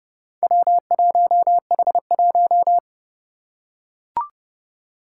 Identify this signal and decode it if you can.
Morse code, heard as a keyed tone, about 31 wpm, 700 Hz. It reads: W1H1
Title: Four Challenging Letter-Number Combinations 30wpm